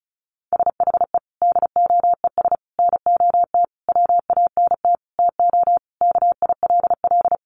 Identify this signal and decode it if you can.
Morse code, heard as a keyed tone, about 34 wpm, 700 Hz. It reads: SHE DOES NOT WANT TO KILL
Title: Easy Phrases 35wpm